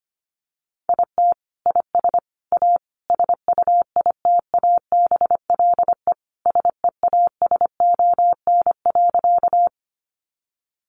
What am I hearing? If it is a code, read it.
Morse code, heard as a keyed tone, about 25 wpm, 700 Hz. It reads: IT IS A SUITABLE SEASON.
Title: Sentences from Top 1400 Words 25wpm